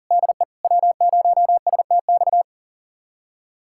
Morse code, heard as a keyed tone, about 40 wpm, 700 Hz. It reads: DE W0STX
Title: USA Callsign DE Callsign - 5 Characters - 40wpm